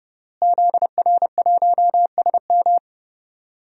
Morse code, heard as a keyed tone, about 30 wpm, 700 Hz. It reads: ZR1SM
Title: DXCC-100 Call Signs - 5 Characters - Repeated 3x - 30wpm